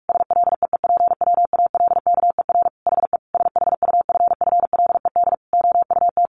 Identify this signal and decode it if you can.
Morse code, heard as a keyed tone, about 44 wpm, 700 Hz. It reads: SLEEPWALKER HE SHUFFLED OUT
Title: A Sound Of Thunder 45wpm